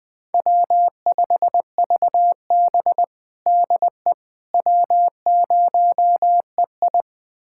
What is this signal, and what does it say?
Morse code, heard as a keyed tone, about 21 wpm, 700 Hz. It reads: W5VB DE W0EI
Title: USA Callsign DE Callsign - 4 Characters - 20wpm